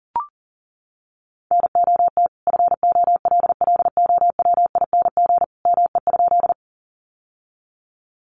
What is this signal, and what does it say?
Morse code, heard as a keyed tone, about 40 wpm, 700 Hz. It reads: NOT FOLLOWING ME?
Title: Sentences from Top 700 Words 40wpm